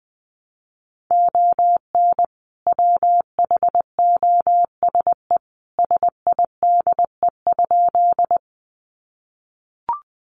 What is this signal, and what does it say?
Morse code, heard as a keyed tone, about 20 wpm, 700 Hz. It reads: ON WHOSE SIDE?
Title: Sentences from Top 1200 Words 20wpm